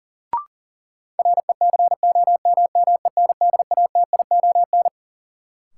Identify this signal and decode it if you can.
Morse code, heard as a keyed tone, about 40 wpm, 700 Hz. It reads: RECOMMENDATION
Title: Sentences from Top 1400 Words - New Word Drill - 40wpm